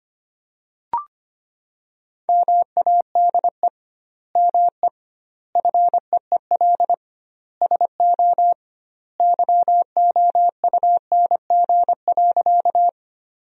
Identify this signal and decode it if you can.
Morse code, heard as a keyed tone, about 25 wpm, 700 Hz. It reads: MADE ME FEEL SO YOUNG.
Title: 2x Word-Spacing - Sentences from Top 400 Words - 25wpm